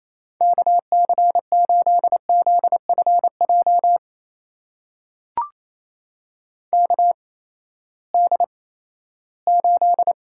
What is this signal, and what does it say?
Morse code, heard as a keyed tone, about 27 wpm, spaced as for 16 wpm, 700 Hz. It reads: KC8ZFJ KD8
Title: USA Call Signs - 6 Characters - 8x Character Spacing - 28wpm